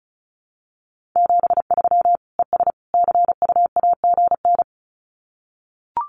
Morse code, heard as a keyed tone, about 34 wpm, 700 Hz. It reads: 73 ES CUAGN
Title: Common QSO Phrases 35wpm